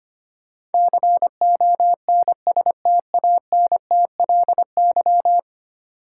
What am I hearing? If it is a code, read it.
Morse code, heard as a keyed tone, about 25 wpm, 700 Hz. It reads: CONSTANTLY
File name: 10 Letter Words - First 3 Letters Given - 25wpm